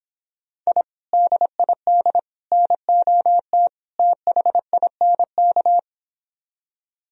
Morse code, heard as a keyed tone, about 29 wpm, 700 Hz. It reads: I DID NOT THINK